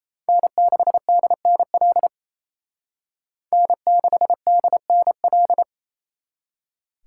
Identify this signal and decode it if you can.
Morse code, heard as a keyed tone, about 35 wpm, 700 Hz. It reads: N6DNL N6DNL